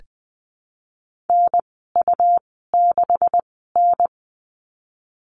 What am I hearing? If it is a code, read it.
Morse code, heard as a keyed tone, about 20 wpm, spaced as for 15 wpm, 700 Hz. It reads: NU6N
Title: 231 Lesson 37g - Letter x - Call Signs 20wpm